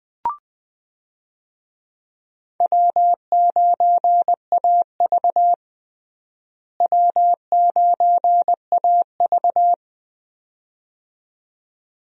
Morse code, heard as a keyed tone, about 20 wpm, 700 Hz. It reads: W9AV W9AV
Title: USA Call Signs - 4 Characters - Repeated 2x - 20wpm